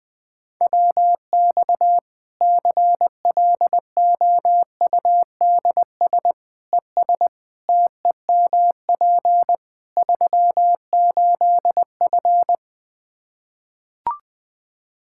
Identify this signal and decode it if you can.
Morse code, heard as a keyed tone, about 20 wpm, 700 Hz. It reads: WX CLOUDS ES TEMP 38F
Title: Practice QSO - Exchange 2 - Info - 20wpm